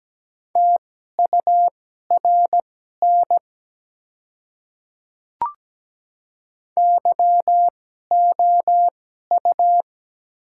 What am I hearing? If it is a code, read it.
Morse code, heard as a keyed tone, about 17 wpm, spaced as for 12 wpm, 690 Hz. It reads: TURN YOU
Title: 2x Character Spacing - Top 300 Words - 17wpm